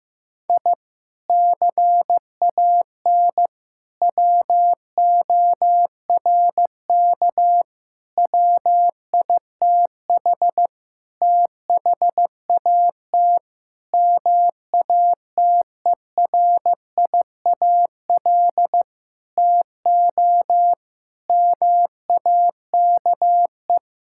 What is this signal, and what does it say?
Morse code, heard as a keyed tone, about 15 wpm, 700 Hz. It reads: I CAN WORK WITH THAT MATERIAL TO MAKE
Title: Sentences from Top 1000 Words - Review Part 5 - 15wpm